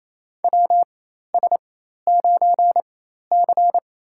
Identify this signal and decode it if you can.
Morse code, heard as a keyed tone, about 28 wpm, spaced as for 13 wpm, 700 Hz. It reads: WS9C
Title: USA Call Signs - 4 Characters - 4x Character Spacing - 28wpm